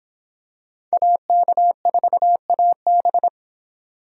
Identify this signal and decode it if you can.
Morse code, heard as a keyed tone, about 25 wpm, 700 Hz. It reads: AK4AB